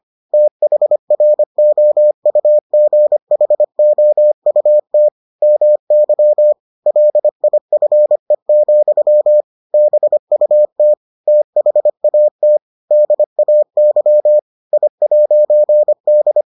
Morse code, heard as a keyed tone, about 25 wpm, 600 Hz. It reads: THROUGHOUT MY LIFE, BUT THAT DAY I'D